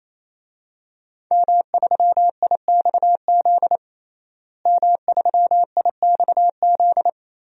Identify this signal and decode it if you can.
Morse code, heard as a keyed tone, about 28 wpm, 700 Hz. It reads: M3IXZ M3IXZ
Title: DXCC-100 Call Signs - 5 Characters - Repeated 3x - 28wpm